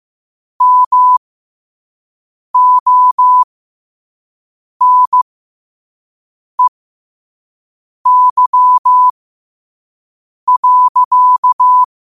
Morse code, heard as a keyed tone, about 15 wpm, spaced as for 5 wpm, 1000 Hz. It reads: MONEY.
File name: qotdfiles / 5wpm